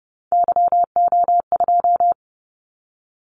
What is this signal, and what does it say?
Morse code, heard as a keyed tone, about 30 wpm, 700 Hz. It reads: YO2
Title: Callsign Prefixes for Top 25 DXCC - 30wpm